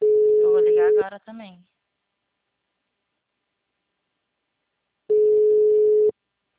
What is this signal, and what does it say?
Ringback tone (Europe, 425 Hz), with speech.